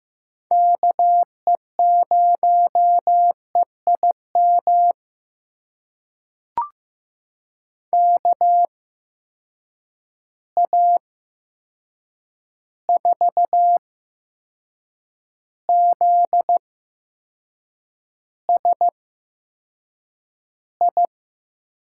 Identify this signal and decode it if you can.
Morse code, heard as a keyed tone, about 15 wpm, spaced as for 6 wpm, 700 Hz. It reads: KE0EIM KA4ZSI
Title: USA Call Signs - 6 Characters - 8x Character Spacing - 15wpm